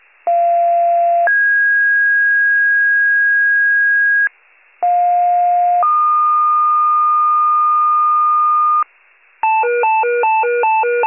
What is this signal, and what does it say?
Two-tone page, heard as a one-second tone, then a three-second tone, about 690 then 1650 Hz, about 710 then 1220 Hz, with noise. It is heard from 0.3 to 8.8 s.
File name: wcalert